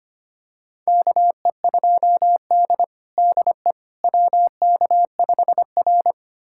Morse code, heard as a keyed tone, about 25 wpm, 700 Hz. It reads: KE2D DE WK5R
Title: USA Callsign DE Callsign - 4 Characters - 25wpm